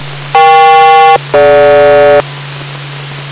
Aircraft SELCAL, heard as two pairs of tones. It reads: EK-DG